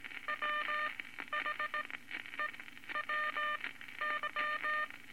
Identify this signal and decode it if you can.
Morse code, heard as a keyed tone, about 19 wpm, spaced as for 12 wpm, 1200 Hz, with noise. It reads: WHEWY
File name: Morse Code Records / 19 A LCM 166 Ten Words Per Minute